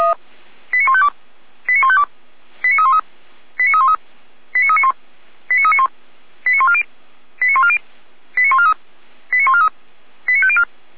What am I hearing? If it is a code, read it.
Five-tone selective call (ZVEI), heard as quick runs of five tones, tones of about 70 ms. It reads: 89152 89152 89313 89313 89391 89391 89140 89140 89143 89143 89504
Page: SDR/ZVEI